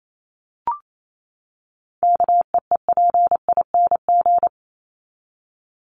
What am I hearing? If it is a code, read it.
Morse code, heard as a keyed tone, about 28 wpm, 700 Hz. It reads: KEEPING
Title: ING Suffix - Top 100 Words with Base Words 28wpm